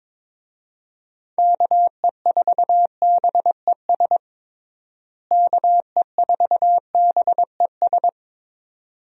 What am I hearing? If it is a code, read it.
Morse code, heard as a keyed tone, about 22 wpm, 700 Hz. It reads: KE4BES KE4BES
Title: USA Call Signs - 6 Characters - Repeated 2x - 22wpm